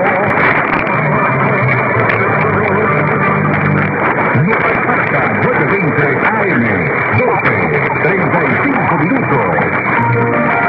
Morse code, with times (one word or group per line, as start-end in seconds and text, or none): none